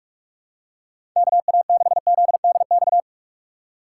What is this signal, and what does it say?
Morse code, heard as a keyed tone, about 47 wpm, 700 Hz. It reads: KA6ZDX
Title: USA Call Signs - 6 Characters - 4x Character Spacing - 45wpm